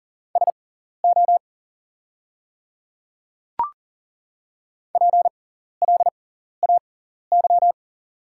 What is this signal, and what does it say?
Morse code, heard as a keyed tone, about 39 wpm, spaced as for 14 wpm, 690 Hz. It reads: SO PLAY
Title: 6x Character Spacing - Top 200 Words - 40wpm